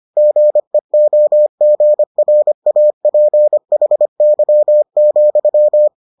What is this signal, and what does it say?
Morse code, heard as a keyed tone, about 25 wpm, 600 Hz. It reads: GEOGRAPHY,